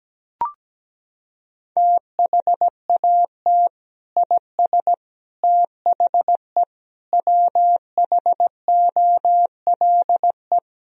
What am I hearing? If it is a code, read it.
Morse code, heard as a keyed tone, about 17 wpm, 700 Hz. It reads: THAT IS THE WHOLE
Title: Sentences from Top 1000 Words - Review Part 3 - 17wpm